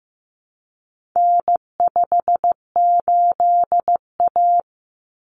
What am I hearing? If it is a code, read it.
Morse code, heard as a keyed tone, about 15 wpm, 700 Hz. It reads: N58A
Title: Common Cut Numbers - 4-Digit 15wpm